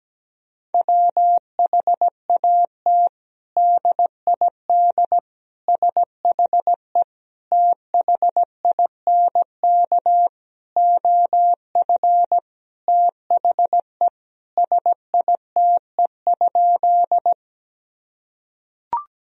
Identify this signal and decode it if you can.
Morse code, heard as a keyed tone, about 17 wpm, 700 Hz. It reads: WHAT DID SHE THINK OF THE SITE?
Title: Sentences from Top 1000 Words - Review Part 4 - 17wpm